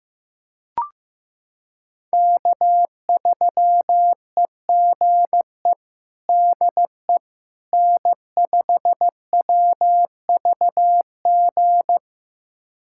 Morse code, heard as a keyed tone, about 16 wpm, 700 Hz. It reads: K3EGE DE N5WVG
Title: USA Callsign DE Callsign - 5 Characters - 15wpm